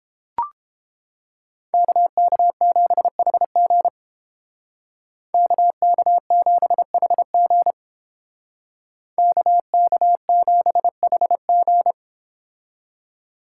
Morse code, heard as a keyed tone, about 29 wpm, 700 Hz. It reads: KK7HG KK7HG KK7HG